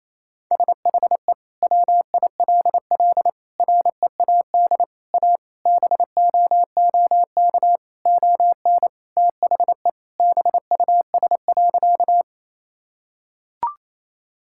Morse code, heard as a keyed tone, about 27 wpm, 700 Hz. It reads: SHE WILL READ A BOOK ON THE BUS.
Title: Sentences from Top 1000 Words - Review Part 6 - 28wpm